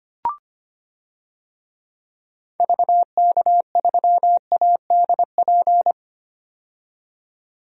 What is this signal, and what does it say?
Morse code, heard as a keyed tone, about 25 wpm, 700 Hz. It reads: VK3ADP
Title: DXCC-100 Call Signs - 6 Characters - 25wpm